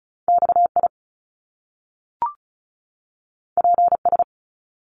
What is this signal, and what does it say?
Morse code, heard as a keyed tone, about 36 wpm, 700 Hz. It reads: XI PS
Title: Top Two Letter Combinations in Words 35wpm